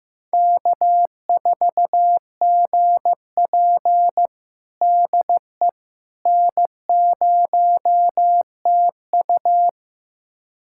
Morse code, heard as a keyed tone, about 15 wpm, 700 Hz. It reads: K4GP DE N0TU
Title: USA Callsign DE Callsign - 4 Characters - 15wpm